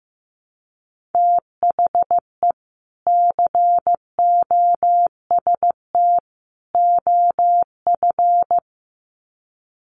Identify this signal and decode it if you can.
Morse code, heard as a keyed tone, about 15 wpm, 700 Hz. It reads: THE COST OF
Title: Sets of 3 Words 15wpm